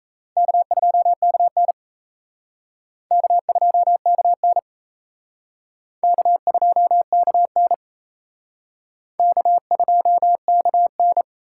Morse code, heard as a keyed tone, about 35 wpm, 700 Hz. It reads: K2KN K2KN K2KN K2KN
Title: Speed Racing - USA 1x2 Call Signs 28wpm